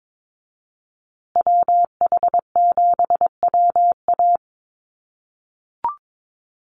Morse code, heard as a keyed tone, about 22 wpm, 700 Hz. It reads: WH7WA
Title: USA Call Signs Repeated 2x 22wpm